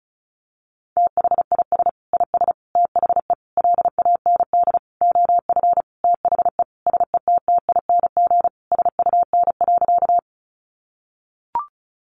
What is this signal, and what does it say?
Morse code, heard as a keyed tone, about 34 wpm, 700 Hz. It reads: THIS IS THE LAND OF THE SETTING SUN.
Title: Sentences from Top 1300 Words 35wpm